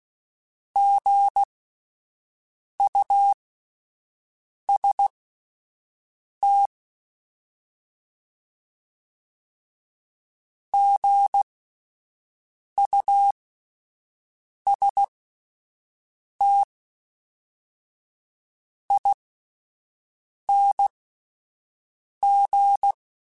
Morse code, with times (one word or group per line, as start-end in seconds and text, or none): 0.8-6.7 GUST
10.7-22.9 GUSTING